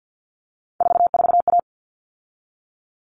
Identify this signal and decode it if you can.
Morse code, heard as a keyed tone, about 50 wpm, 700 Hz. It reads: 44A
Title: All Cut Numbers - 3-Digit 50wpm